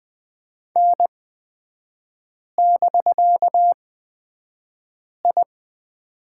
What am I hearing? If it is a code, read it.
Morse code, heard as a keyed tone, about 20 wpm, spaced as for 5 wpm, 700 Hz. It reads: N<BK>I
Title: 250 Lesson 40e - BK - Three Characters 20wpm